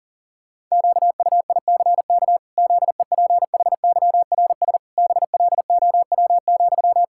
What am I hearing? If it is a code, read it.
Morse code, heard as a keyed tone, about 40 wpm, 700 Hz. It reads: QUICK ZEPHYRS BLOW,
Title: Pangram Challenge v2 - 40wpm